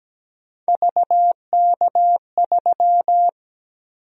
Morse code, heard as a keyed tone, about 17 wpm, 700 Hz. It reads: VK3
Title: USA Call Sign Prefixes - Repeated 2x - 17wpm